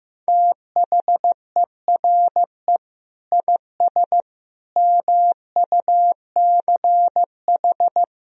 Morse code, heard as a keyed tone, about 15 wpm, 700 Hz. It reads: THERE IS MUCH
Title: Sentences from Top 1000 Words - Review Part 4 - 15wpm